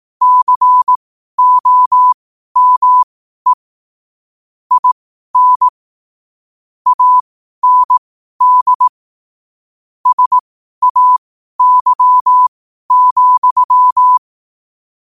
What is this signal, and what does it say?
Morse code, heard as a keyed tone, about 18 wpm, spaced as for 12 wpm, 1000 Hz. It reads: COME IN AND SAY,
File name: 12wpm